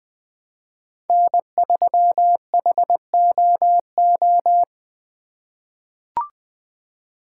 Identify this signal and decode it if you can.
Morse code, heard as a keyed tone, about 20 wpm, 700 Hz. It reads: N3HOO
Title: USA Call Signs - 5 Characters - 4x Character Spacing - 20wpm